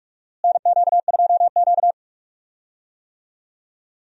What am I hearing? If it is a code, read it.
Morse code, heard as a keyed tone, about 45 wpm, 700 Hz. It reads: NQ2Q
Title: USA Call Signs - 4 Characters - Repeated 3x - 45wpm